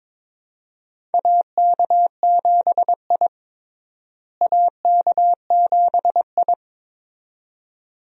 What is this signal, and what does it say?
Morse code, heard as a keyed tone, about 22 wpm, 700 Hz. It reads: AK7I AK7I
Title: USA Call Signs - 4 Characters - Repeated 2x - 22wpm